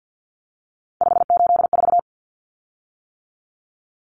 Morse code, heard as a keyed tone, about 50 wpm, 700 Hz. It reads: 584
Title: Common Cut Numbers - 3-Digit 50wpm